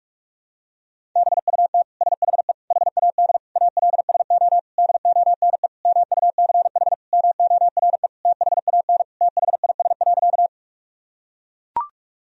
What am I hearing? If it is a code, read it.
Morse code, heard as a keyed tone, about 41 wpm, 700 Hz. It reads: BUT SHE HAD ALSO DONE MUCH MORE THAN THIS.
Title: Sentences from Top 1000 Words - Review Part 1 - 45wpm